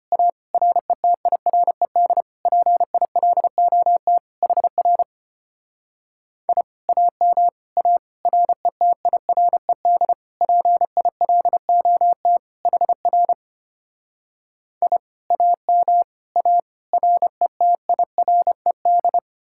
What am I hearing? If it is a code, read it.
Morse code, heard as a keyed tone, about 33 wpm, 700 Hz. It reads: A RETIRED PILOT HR I AM A RETIRED PILOT HR I AM A RETIRED